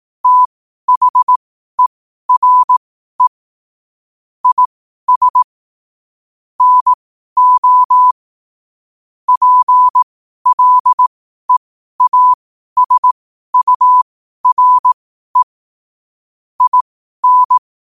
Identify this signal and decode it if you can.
Morse code, heard as a keyed tone, about 18 wpm, spaced as for 12 wpm, 1000 Hz. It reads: THERE IS NO PLEASURE IN
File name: qotdfiles / 12wpm